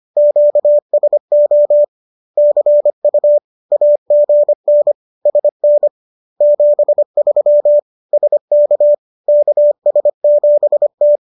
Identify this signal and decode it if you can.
Morse code, heard as a keyed tone, about 25 wpm, 590 Hz. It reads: QSO CU AGN SN 73 SK KS7T